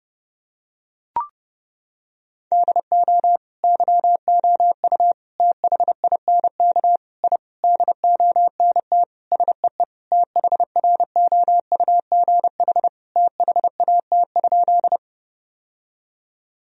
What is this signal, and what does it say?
Morse code, heard as a keyed tone, about 30 wpm, 700 Hz. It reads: DO YOU THINK I DONT SEE THROUGH THAT?
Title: Sentences from Top 1000 Words - Review Part 1 - 30wpm